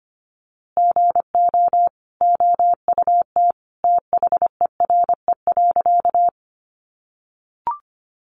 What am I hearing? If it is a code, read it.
Morse code, heard as a keyed tone, about 25 wpm, 700 Hz. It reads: GO OUT THERE.
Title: A Sound Of Thunder 25wpm